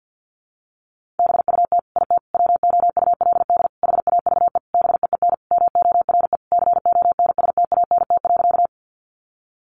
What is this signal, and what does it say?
Morse code, heard as a keyed tone, about 50 wpm, 700 Hz. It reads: BUT IT WOULD HAVE BEEN MORE CONSTANT.
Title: Sentences from Top 1500 Words 50wpm